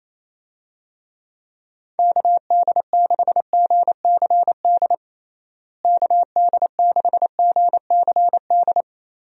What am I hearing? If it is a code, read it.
Morse code, heard as a keyed tone, about 28 wpm, 700 Hz. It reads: KD6GCD KD6GCD
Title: USA Call Signs - 6 Characters - Repeated 2x - 28wpm